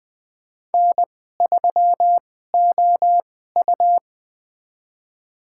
Morse code, heard as a keyed tone, about 21 wpm, spaced as for 15 wpm, 700 Hz. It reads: N3OU
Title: USA Call Signs - 2x Character Spacing - 20wpm